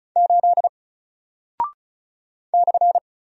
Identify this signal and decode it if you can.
Morse code, heard as a keyed tone, about 36 wpm, 700 Hz. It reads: 8 /
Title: Single Character (Letter:Number:Punctuation:Prosign) 35wpm